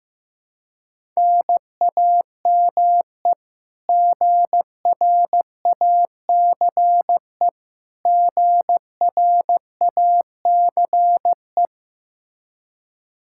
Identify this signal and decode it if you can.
Morse code, heard as a keyed tone, about 16 wpm, 700 Hz. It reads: NAME GRACE GRACE
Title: Practice QSO - Exchange 1 - Beginning - 15wpm